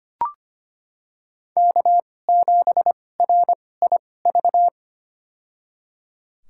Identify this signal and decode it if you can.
Morse code, heard as a keyed tone, about 25 wpm, spaced as for 18 wpm, 700 Hz. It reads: K7RIV
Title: USA Call Signs - 5 Characters - 2x Character Spacing - 25wpm